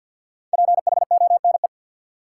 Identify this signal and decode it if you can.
Morse code, heard as a keyed tone, about 53 wpm, 700 Hz. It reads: PHONE
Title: Sentences from Top 1100 Words - New Word Drill Alt - 50wpm